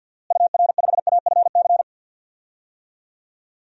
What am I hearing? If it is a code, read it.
Morse code, heard as a keyed tone, about 53 wpm, 700 Hz. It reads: UR5AFC
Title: DXCC-100 Call Signs - 6 Characters - Repeated 2x - 50wpm